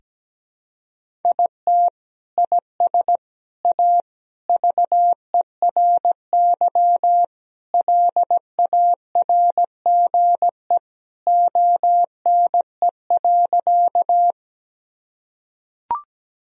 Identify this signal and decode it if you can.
Morse code, heard as a keyed tone, about 17 wpm, 700 Hz. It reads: IT IS A VERY LARGE ONE.
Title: Sentences from Top 1000 Words - Review Part 1 - 17wpm